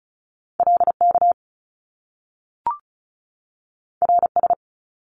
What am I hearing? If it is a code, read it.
Morse code, heard as a keyed tone, about 36 wpm, 700 Hz. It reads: LK RS
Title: Top Two Letter Combinations in Words 35wpm